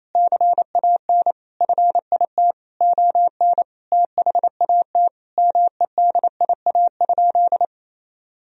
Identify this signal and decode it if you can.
Morse code, heard as a keyed tone, about 28 wpm, 700 Hz. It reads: CAN FIT ON THAT MEDIA?
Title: Sentences from Top 1000 Words - Review Part 4 - 28wpm